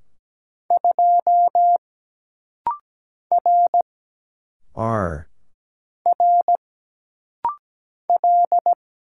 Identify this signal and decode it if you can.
Morse code, heard as a keyed tone, about 17 wpm, 700 Hz. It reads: 2 R R L
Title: Single Letter-Number Alt - Rapid-Fire 17wpm